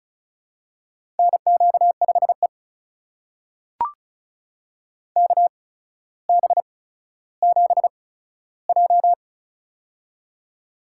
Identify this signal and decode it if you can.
Morse code, heard as a keyed tone, about 36 wpm, spaced as for 15 wpm, 700 Hz. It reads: NQ5E KB7J
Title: USA Call Signs - 4 Characters - 8x Character Spacing - 35wpm